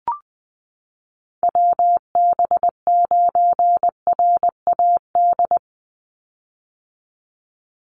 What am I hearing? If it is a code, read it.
Morse code, heard as a keyed tone, about 20 wpm, 700 Hz. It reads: WB9RAD
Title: USA Call Signs 20wpm